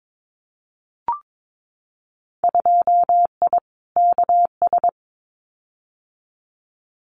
Morse code, heard as a keyed tone, about 22 wpm, 700 Hz. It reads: 2I KS
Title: Winter Field Day Exchanges Encore 22wpm